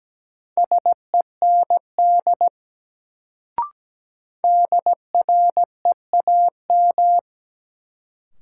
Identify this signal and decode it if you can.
Morse code, heard as a keyed tone, about 17 wpm, 700 Hz. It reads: SEND DREAM
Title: Sentences from Top 1500 Words - New Word Drill - 17wpm